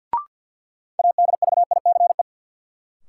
Morse code, heard as a keyed tone, about 47 wpm, 700 Hz. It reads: ADVICE
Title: Sentences from Top 1100 Words - New Word Drill - 50wpm